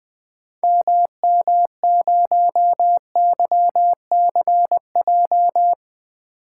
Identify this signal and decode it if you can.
Morse code, heard as a keyed tone, about 20 wpm, 700 Hz. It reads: MM0YCJ
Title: 2023 SOTA Activator Call Signs - 20wpm